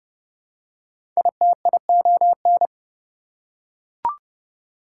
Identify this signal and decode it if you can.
Morse code, heard as a keyed tone, about 30 wpm, 700 Hz. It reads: ITION
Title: Prefixes and Suffixes 30wpm